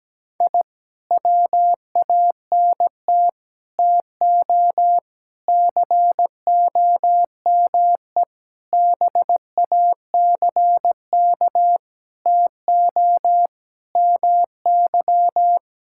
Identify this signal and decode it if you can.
Morse code, heard as a keyed tone, about 17 wpm, 700 Hz. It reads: I WANT TO COME BACK TO MY
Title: Sentences from Top 1000 Words - Review Part 6 - 17wpm